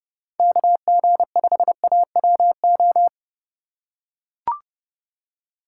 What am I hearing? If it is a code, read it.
Morse code, heard as a keyed tone, about 31 wpm, 700 Hz. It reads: KG5AWO
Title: USA Call Signs - 3x Character Spacing - 30wpm